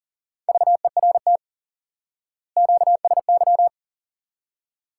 Morse code, heard as a keyed tone, about 40 wpm, 700 Hz. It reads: VERT QSY
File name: IQE Course 3 - Lesson 89 - YAGI - Two Elements - 40wpm